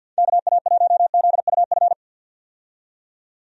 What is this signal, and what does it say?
Morse code, heard as a keyed tone, about 50 wpm, 700 Hz. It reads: KA1ZUF
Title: USA Call Signs - 6 Characters - 50wpm